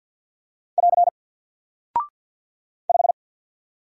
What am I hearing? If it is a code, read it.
Morse code, heard as a keyed tone, about 49 wpm, 700 Hz. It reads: <AR> 5
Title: Single Character (Letter:Number:Punctuation:Prosign) 50wpm